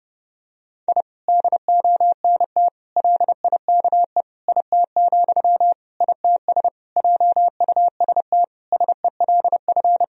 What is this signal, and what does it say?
Morse code, heard as a keyed tone, about 30 wpm, 700 Hz. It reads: I DONT LIKE IT, ITS JUST SELF
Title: Sentences from Top 1000 Words - Review Part 1 - 30wpm